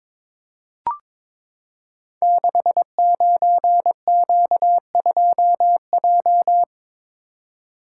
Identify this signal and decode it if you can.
Morse code, heard as a keyed tone, about 22 wpm, 700 Hz. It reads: 69Q2J
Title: Code Groups - 5 Letter:Numbers - 22wpm